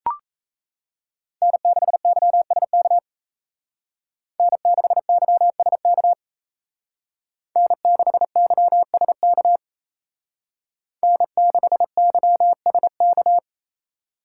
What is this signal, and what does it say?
Morse code, heard as a keyed tone, about 35 wpm, 700 Hz. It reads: N6YSK N6YSK N6YSK N6YSK